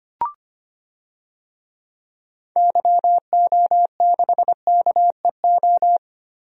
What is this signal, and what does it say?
Morse code, heard as a keyed tone, about 25 wpm, 700 Hz. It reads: YO6KEO
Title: DXCC-100 Call Signs - 6 Characters - 25wpm